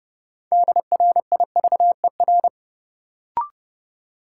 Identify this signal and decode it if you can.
Morse code, heard as a keyed tone, about 30 wpm, 700 Hz. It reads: DRIVER
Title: Sentences from Top 1000 Words - New Word Drill - 30wpm